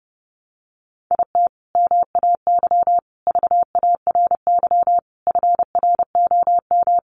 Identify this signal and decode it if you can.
Morse code, heard as a keyed tone, about 28 wpm, 700 Hz. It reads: IT MAY VARY FROM
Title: Sentences from Top 1200 Words 30wpm